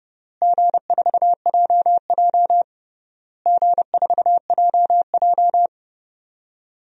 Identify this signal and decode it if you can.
Morse code, heard as a keyed tone, about 31 wpm, 700 Hz. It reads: G4JJ G4JJ
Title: DXCC-100 Call Signs - 4 Characters - Repeated 3x - 30wpm